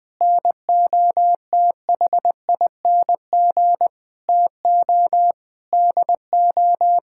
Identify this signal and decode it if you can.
Morse code, heard as a keyed tone, about 20 wpm, 700 Hz. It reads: NOTHING TO DO
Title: Sentences with 7-Word Phrases - 20wpm